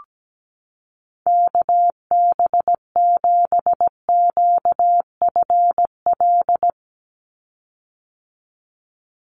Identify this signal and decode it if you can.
Morse code, heard as a keyed tone, about 17 wpm, 700 Hz. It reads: KB7QFL
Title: USA Call Signs 17wpm